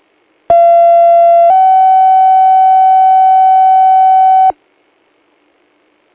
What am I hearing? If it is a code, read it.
Two-tone page, heard as a one-second tone, then a three-second tone, about 680 then 760 Hz. It is heard from 0.5 to 4.5 s.